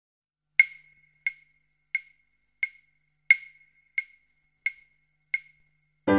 There is no code.